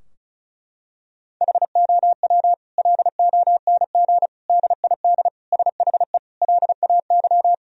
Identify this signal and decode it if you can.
Morse code, heard as a keyed tone, about 36 wpm, 700 Hz. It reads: HOW LONG DID SHE LAY
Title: Sentences from Top 1000 Words - Review Part 2 - 35wpm